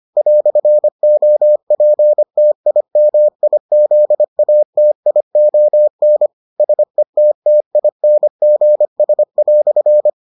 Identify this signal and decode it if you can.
Morse code, heard as a keyed tone, about 25 wpm, 600 Hz. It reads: "OPTIMIZATION SETTINGS"